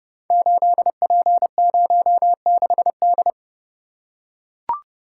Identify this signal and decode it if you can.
Morse code, heard as a keyed tone, about 30 wpm, 700 Hz. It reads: 8P06D
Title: Code Groups - 5 Letter:Numbers - 30wpm